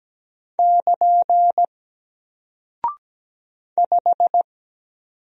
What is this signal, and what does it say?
Morse code, heard as a keyed tone, about 17 wpm, 700 Hz. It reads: <KN> 5
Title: Single Character (Letter:Number:Punctuation:Prosign) 17wpm